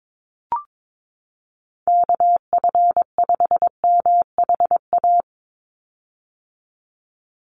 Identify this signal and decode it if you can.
Morse code, heard as a keyed tone, about 22 wpm, 700 Hz. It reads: KF5MHA
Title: USA Call Signs 22wpm